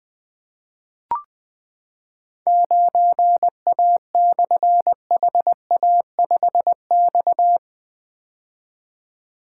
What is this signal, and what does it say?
Morse code, heard as a keyed tone, about 20 wpm, 700 Hz. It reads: 9A/HA5X
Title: Difficult Call Signs 20wpm